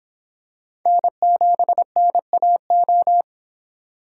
Morse code, heard as a keyed tone, about 26 wpm, 700 Hz. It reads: N7NAO